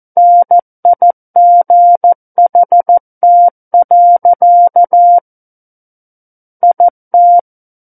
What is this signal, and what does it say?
Morse code, heard as a keyed tone, about 14 wpm, 700 Hz. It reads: NIGHT. IT
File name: Book One / 05-Chapter 5 - 14 WPM